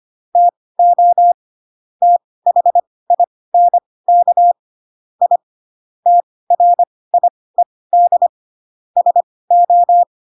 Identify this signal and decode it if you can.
Morse code, heard as a keyed tone, about 25 wpm, spaced as for 18 wpm, 700 Hz. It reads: TO THINK I TRIED SO
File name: Pinocchio ch3 12 25 ik1hge